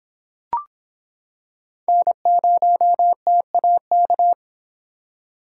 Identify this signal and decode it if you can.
Morse code, heard as a keyed tone, about 26 wpm, 700 Hz. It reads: N0TAK